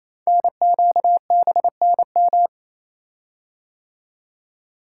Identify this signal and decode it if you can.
Morse code, heard as a keyed tone, about 27 wpm, 700 Hz. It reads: NQBNM
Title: Code Groups - 5 Letters - 28wpm